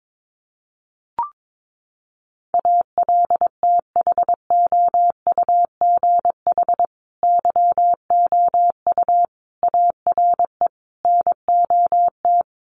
Morse code, heard as a keyed tone, about 22 wpm, 700 Hz. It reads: ALTHOUGH YOU ARE NOT
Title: Sentences from Top 1100 Words 22wpm